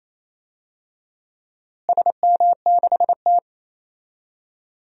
Morse code, heard as a keyed tone, about 27 wpm, 700 Hz. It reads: SM6T
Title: DXCC-100 Call Signs - 4 Characters - Repeated 3x - 28wpm